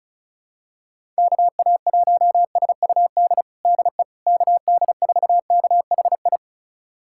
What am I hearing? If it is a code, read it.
Morse code, heard as a keyed tone, about 36 wpm, 700 Hz. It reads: KA1SUD DE KD4KHI
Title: USA Callsign DE Callsign - 6 Characters - 35wpm